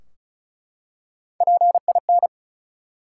Morse code, heard as a keyed tone, about 36 wpm, 700 Hz. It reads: PIN
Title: Three Letter Words - 35wpm